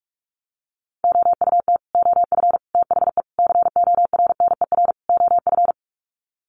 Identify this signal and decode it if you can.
Morse code, heard as a keyed tone, about 44 wpm, 700 Hz. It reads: OUT OF THE CORNER OF
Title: Sets of 5 Words 45wpm